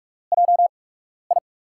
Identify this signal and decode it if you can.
Morse code, heard as a keyed tone, about 42 wpm, spaced as for 12 wpm, 700 Hz. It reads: JI
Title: USA Call Signs - 6 Characters - 8x Character Spacing - 45wpm